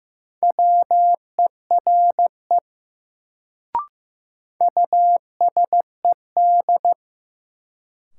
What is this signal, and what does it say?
Morse code, heard as a keyed tone, about 15 wpm, 700 Hz. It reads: WERE USED
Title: Sets of 2 Words - New Word Drill - 15wpm